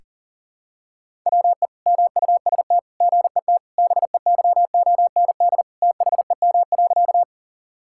Morse code, heard as a keyed tone, about 40 wpm, 690 Hz. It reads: WE MUST GET BEYOND THEM.
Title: Sentences from Top 1000 Words - Review Part 6 - 40wpm